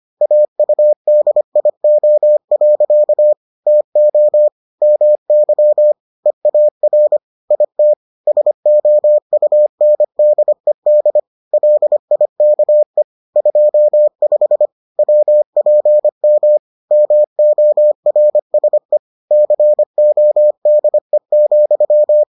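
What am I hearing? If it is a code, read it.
Morse code, heard as a keyed tone, about 25 wpm, 600 Hz. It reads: AUDIO. TO MY EAR IT SOUNDED LIKE 25 WPM MORSE CODE,
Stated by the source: Morse edition of the Foundations of Amateur Radio episodes.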